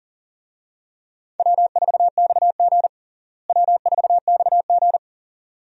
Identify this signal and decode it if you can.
Morse code, heard as a keyed tone, about 40 wpm, 700 Hz. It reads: W4XG W4XG
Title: USA Call Signs - 4 Characters - Repeated 2x - 40wpm